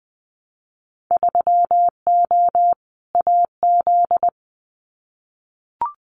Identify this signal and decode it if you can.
Morse code, heard as a keyed tone, about 20 wpm, 700 Hz. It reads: 3O AZ
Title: Winter Field Day Exchanges 20wpm